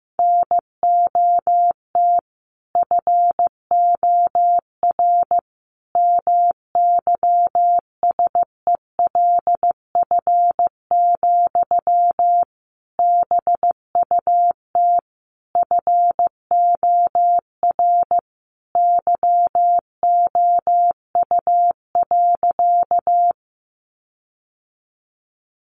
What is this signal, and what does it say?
Morse code, heard as a keyed tone, about 15 wpm, 700 Hz. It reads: NOT FOR MYSELF, BUT FOR YOU.
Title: Sentences from Top 1300 Words 15wpm